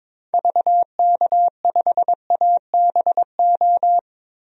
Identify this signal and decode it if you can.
Morse code, heard as a keyed tone, about 22 wpm, 700 Hz. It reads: VK5ABO
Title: DXCC-100 Call Signs - 6 Characters - 22wpm